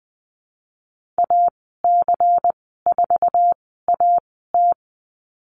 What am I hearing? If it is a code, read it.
Morse code, heard as a keyed tone, about 19 wpm, spaced as for 14 wpm, 700 Hz. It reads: AC4AT
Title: 193 Lesson 31g - Letter v - Call Signs 20wpm